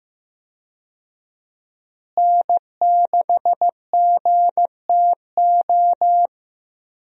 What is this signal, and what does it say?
Morse code, heard as a keyed tone, about 15 wpm, 700 Hz. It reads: N6GTO
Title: USA Call Signs - 5 Characters - Repeated 3x - 15wpm